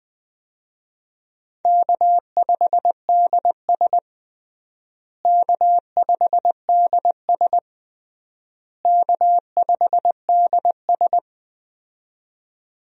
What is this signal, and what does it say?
Morse code, heard as a keyed tone, about 20 wpm, 700 Hz. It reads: K5DS K5DS K5DS
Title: USA Call Signs - 4 Characters - Repeated 3x - 20wpm